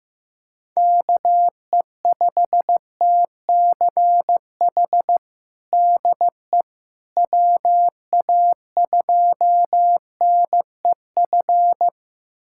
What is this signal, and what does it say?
Morse code, heard as a keyed tone, about 15 wpm, 700 Hz. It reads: KE5TCH DE WA2NEF
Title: USA Callsign DE Callsign 15wpm